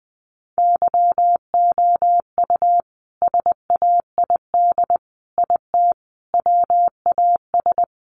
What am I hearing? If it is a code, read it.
Morse code, heard as a keyed tone, about 20 wpm, 700 Hz. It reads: YOU SAID IT WAS
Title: Sentences from Top 1100 Words 20wpm